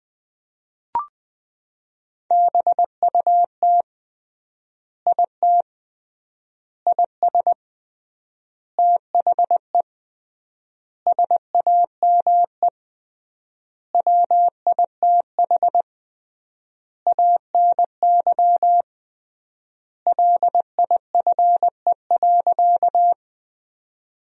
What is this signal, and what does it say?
Morse code, heard as a keyed tone, about 20 wpm, 700 Hz. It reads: BUT IT IS THE SAME WITH ANY LIFE.
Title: 3x Word-Spacing - Sentences from Top 300 Words - 20wpm